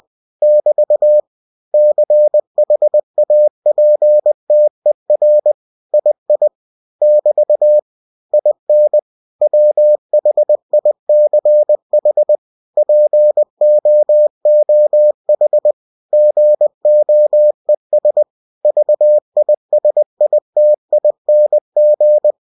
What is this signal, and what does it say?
Morse code, heard as a keyed tone, about 20 wpm, 600 Hz. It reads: <BT> CHAPTER II <BT> IN WHICH POOH GOES VISITING